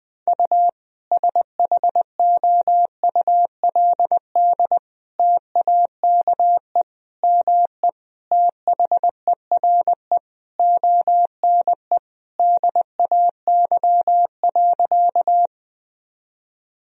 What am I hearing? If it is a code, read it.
Morse code, heard as a keyed tone, about 20 wpm, 700 Hz. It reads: U SHOULD TAKE ME THERE ONE DAY.
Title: Sentences from Top 300 Words and Abbreviations 20wpm